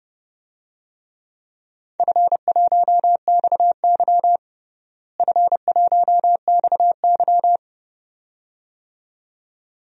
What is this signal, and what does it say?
Morse code, heard as a keyed tone, about 30 wpm, 700 Hz. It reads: F1XY F1XY
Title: DXCC-100 Call Signs - 4 Characters - Repeated 2x - 30wpm